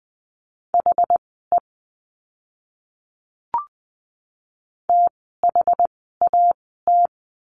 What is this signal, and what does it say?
Morse code, heard as a keyed tone, about 20 wpm, spaced as for 14 wpm, 700 Hz. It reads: HE THAT
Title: 061 Lesson 11f - Letter h - Words 20wpm